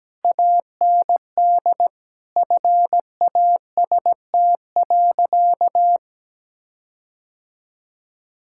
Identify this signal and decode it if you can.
Morse code, heard as a keyed tone, about 17 wpm, 700 Hz. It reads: AND FAST.
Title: Sentences from Top 1600 Words 17wpm